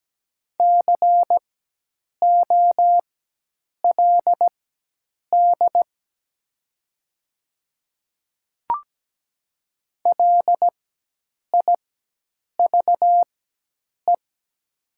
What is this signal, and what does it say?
Morse code, heard as a keyed tone, about 17 wpm, spaced as for 8 wpm, 700 Hz. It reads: COLD LIVE
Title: 4x Character Spacing - Top 500 Words - 17wpm